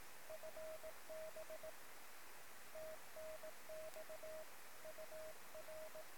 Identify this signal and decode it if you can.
Morse code, heard as a keyed tone, about 19 wpm, 630 Hz, with noise. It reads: FB TNX UR
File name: sm7lcb / 1 oh2pu-p 13cm